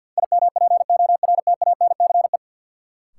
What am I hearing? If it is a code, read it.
Morse code, heard as a keyed tone, about 49 wpm, 700 Hz. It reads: IMPORTANCE
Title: Sentences from Top 1200 Words - New Word Drill - 50wpm